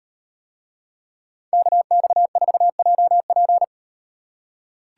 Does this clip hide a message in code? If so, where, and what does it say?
Yes: Morse code from 1.5 to 3.6 s, heard as a keyed tone, about 38 wpm, 700 Hz. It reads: KX4JP